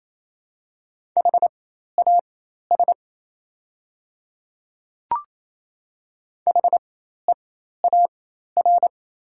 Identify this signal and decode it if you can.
Morse code, heard as a keyed tone, about 27 wpm, spaced as for 13 wpm, 700 Hz. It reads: HAS HEAR
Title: 4x Character Spacing - Top 300 Words - 28wpm